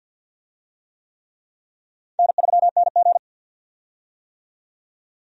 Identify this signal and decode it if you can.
Morse code, heard as a keyed tone, about 52 wpm, 700 Hz. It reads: N3NG
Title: USA Call Signs - 4 Characters - 50wpm